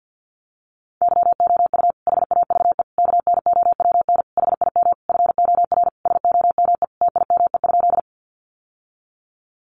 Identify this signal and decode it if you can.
Morse code, heard as a keyed tone, about 48 wpm, 700 Hz. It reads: YOU HAVE KNOWN HIM FOR SOME TIME?
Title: Sentences from Top 600 Words 50wpm